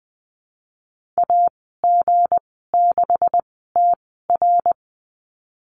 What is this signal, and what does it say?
Morse code, heard as a keyed tone, about 20 wpm, spaced as for 15 wpm, 700 Hz. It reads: AG6TR
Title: 186 Lesson 30g - Letter b - Call Signs 20wpm